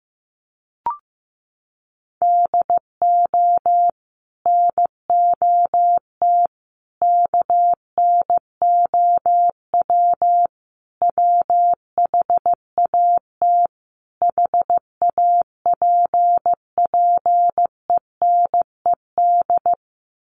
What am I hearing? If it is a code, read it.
Morse code, heard as a keyed tone, about 15 wpm, 700 Hz. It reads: DO NOT KNOW WHAT HAPPENED
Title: Sets of 5 Words Encore2 15wpm